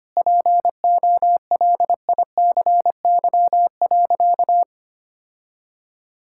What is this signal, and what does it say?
Morse code, heard as a keyed tone, about 25 wpm, 700 Hz. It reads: POLICY.
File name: practice / 25wpm / Difficult Sentences with Hints - 25wpm